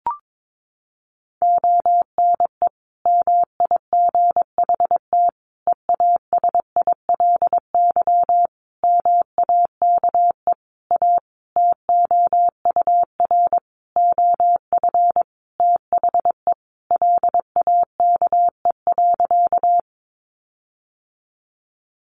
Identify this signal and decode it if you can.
Morse code, heard as a keyed tone, about 22 wpm, 700 Hz. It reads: ONE MIGHT EASILY MAKE A TOUR OF THE LAKE.
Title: Sentences from Top 1400 Words 22wpm